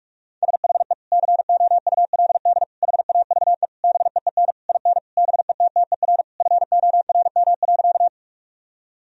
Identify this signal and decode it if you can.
Morse code, heard as a keyed tone, about 46 wpm, 700 Hz. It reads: SHE COULD HAVE BEEN IN BETTER FORM.
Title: Sentences from Top 1000 Words - Review Part 1 - 45wpm